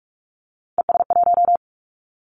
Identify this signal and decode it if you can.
Morse code, heard as a keyed tone, about 46 wpm, 700 Hz. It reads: ES1
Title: Callsign Prefixes for Top 100 DXCC - 45wpm